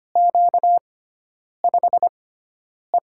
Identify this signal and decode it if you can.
Morse code, heard as a keyed tone, about 25 wpm, spaced as for 9 wpm, 700 Hz. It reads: Q5E
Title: USA Call Signs - 4 Characters - 6x Character Spacing - 25wpm